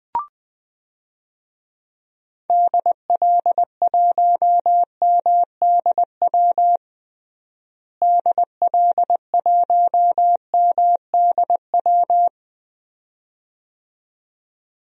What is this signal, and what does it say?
Morse code, heard as a keyed tone, about 20 wpm, 700 Hz. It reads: DL1MDW DL1MDW
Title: DXCC-100 Call Signs - 6 Characters - Repeated 2x - 20wpm